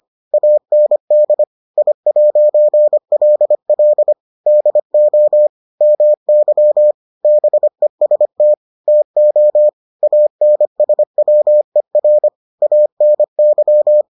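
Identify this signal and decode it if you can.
Morse code, heard as a keyed tone, about 25 wpm, 600 Hz. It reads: AND I'LL DO MY BEST TO ANSWER ANY
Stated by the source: Morse edition of the Foundations of Amateur Radio episodes.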